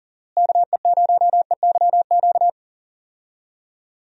Morse code, heard as a keyed tone, about 39 wpm, 700 Hz. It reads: KE0EYQ
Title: USA Call Signs - 6 Characters - Repeated 2x - 40wpm